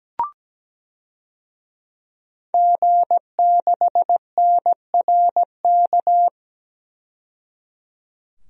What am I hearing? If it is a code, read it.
Morse code, heard as a keyed tone, about 17 wpm, 700 Hz. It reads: G6NRK
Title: DXCC-100 Call Signs - 5 Characters - 17wpm